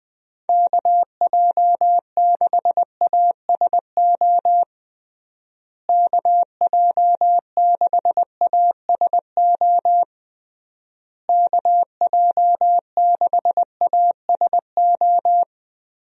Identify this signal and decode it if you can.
Morse code, heard as a keyed tone, about 21 wpm, 700 Hz. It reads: KJ6ASO KJ6ASO KJ6ASO
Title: USA Call Signs - 6 Characters - Repeated 3x - 20wpm